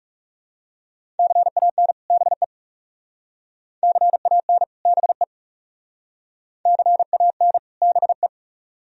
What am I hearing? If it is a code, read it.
Morse code, heard as a keyed tone, about 36 wpm, 700 Hz. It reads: CAN BE CAN BE CAN BE
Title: Speed-Racing - Sets of 2 Words 30wpm